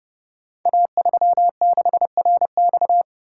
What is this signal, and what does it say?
Morse code, heard as a keyed tone, about 30 wpm, 700 Hz. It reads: A36RX
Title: Code Groups - 5 Letter:Numbers - 30wpm